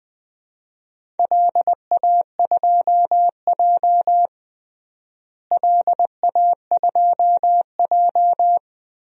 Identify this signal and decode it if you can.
Morse code, heard as a keyed tone, about 20 wpm, 700 Hz. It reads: LA2J LA2J
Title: DXCC-100 Call Signs - 4 Characters - Repeated 3x - 20wpm